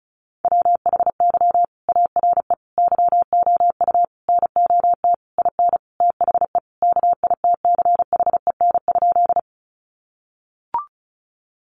Morse code, heard as a keyed tone, about 34 wpm, 700 Hz. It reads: WHY ARE YOU NOT IN THE KITCHEN?
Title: Sentences from Top 1400 Words 35wpm